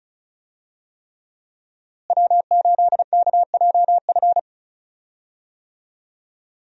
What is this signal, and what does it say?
Morse code, heard as a keyed tone, about 36 wpm, 700 Hz. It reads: W8KJF
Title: USA Call Signs - 5 Characters - Repeated 3x - 35wpm